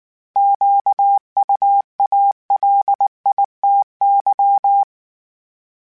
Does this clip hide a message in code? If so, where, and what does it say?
Yes: Morse code from 0.4 to 4.8 s, heard as a keyed tone, about 19 wpm, 800 Hz. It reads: QUALITY